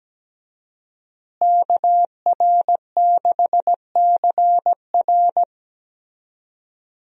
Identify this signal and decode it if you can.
Morse code, heard as a keyed tone, about 17 wpm, 700 Hz. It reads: KR6CR
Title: USA Call Signs - 5 Characters - Repeated 3x - 17wpm